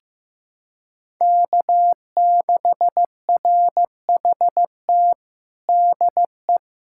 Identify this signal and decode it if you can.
Morse code, heard as a keyed tone, about 15 wpm, 700 Hz. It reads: K6RHT DE
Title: K1USN SST QSO Practice 15wpm